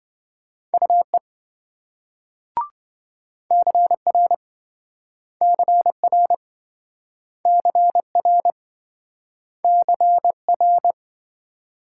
Morse code, heard as a keyed tone, about 26 wpm, 700 Hz. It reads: UE CR CR CR CR